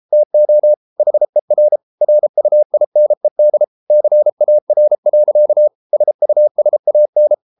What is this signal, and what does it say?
Morse code, heard as a keyed tone, about 33 wpm, 600 Hz. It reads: TO HER RUINED CAR. SUSAN